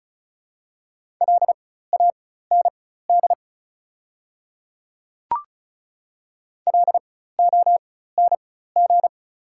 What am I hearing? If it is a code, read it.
Morse code, heard as a keyed tone, about 34 wpm, spaced as for 16 wpm, 700 Hz. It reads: LAND LONG
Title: 4x Character Spacing - Top 500 Words - 35wpm